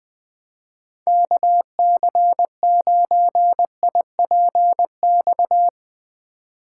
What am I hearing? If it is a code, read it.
Morse code, heard as a keyed tone, about 20 wpm, 700 Hz. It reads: KC9IPX
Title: USA Call Signs - 6 Characters - 3x Character Spacing - 20wpm